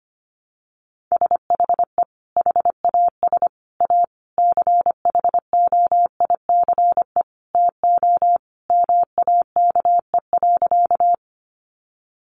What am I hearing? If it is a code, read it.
Morse code, heard as a keyed tone, about 25 wpm, 700 Hz. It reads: SHE HAS A CHOICE TO MAKE.
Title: Sentences from Top 900 Words 25wpm